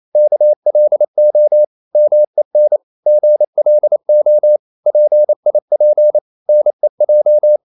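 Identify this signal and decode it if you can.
Morse code, heard as a keyed tone, about 28 wpm, 600 Hz. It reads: KLO MEN GLO PIP NEJ
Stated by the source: Danish short words sent in Morse at 28 wpm, from CW med Gnister.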